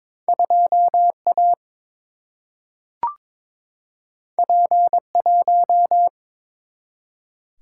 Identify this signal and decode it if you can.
Morse code, heard as a keyed tone, about 23 wpm, 700 Hz. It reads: 2A P1
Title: Single-Letter Single-Number 22wpm